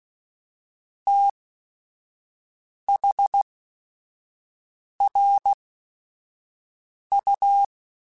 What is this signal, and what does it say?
Morse code, heard as a keyed tone, about 16 wpm, spaced as for 5 wpm, 800 Hz. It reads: THRU
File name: QSOs / QSOs-mp3s / WD0FBT 1@5